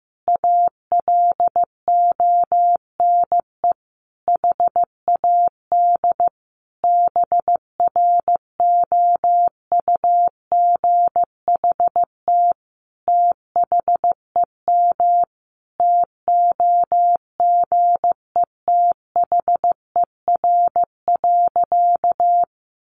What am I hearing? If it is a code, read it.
Morse code, heard as a keyed tone, about 15 wpm, 700 Hz. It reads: ALONE HAD BROUGHT THEM TOGETHER.
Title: Sentences from Top 1000 Words 15wpm